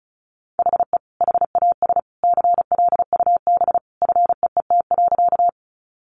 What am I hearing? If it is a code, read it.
Morse code, heard as a keyed tone, about 34 wpm, 700 Hz. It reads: HE HAS CLUB FEET.
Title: Sentences from Top 1200 Words 35wpm